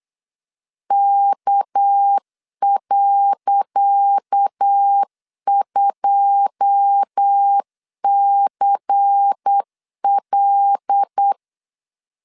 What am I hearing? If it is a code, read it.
Morse code, heard as a keyed tone, about 8 wpm, 800 Hz. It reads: K.2CL
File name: LEZIONI / 10 SIGNS / 10C DOT 8wpm 3649